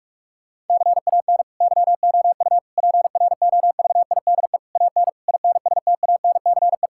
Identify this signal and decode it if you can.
Morse code, heard as a keyed tone, about 46 wpm, 700 Hz. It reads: CAN YOU PROVIDE AN INSTANCE
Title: Sentences from Top 1000 Words - Review Part 5 - 45wpm